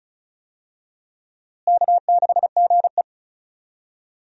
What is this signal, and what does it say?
Morse code, heard as a keyed tone, about 36 wpm, 700 Hz. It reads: K6GE
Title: USA Call Signs - 4 Characters - Repeated 3x - 35wpm